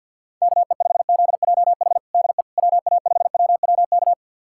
Morse code, heard as a keyed tone, about 49 wpm, 700 Hz. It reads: KE5ZJH DE WA5WWK
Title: USA Callsign DE Callsign - 6 Characters - 50wpm